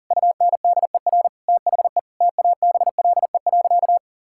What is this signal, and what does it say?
Morse code, heard as a keyed tone, about 37 wpm, 700 Hz. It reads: UNDER THE TABLE.
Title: Sentences from Top 1000 Words - Review Part 5 - 40wpm